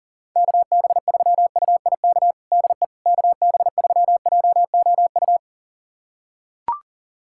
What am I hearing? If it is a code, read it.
Morse code, heard as a keyed tone, about 37 wpm, 700 Hz. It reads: KB3UIK DE KB3JOU
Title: USA Callsign DE Callsign - 6 Characters - 40wpm